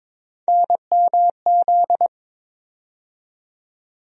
Morse code, heard as a keyed tone, about 22 wpm, 700 Hz. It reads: NMZ
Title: USA Call Sign Postfixes - 3 Letters 22wpm